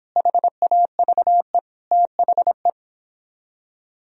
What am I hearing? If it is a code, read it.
Morse code, heard as a keyed tone, about 30 wpm, 700 Hz. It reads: HAVE THE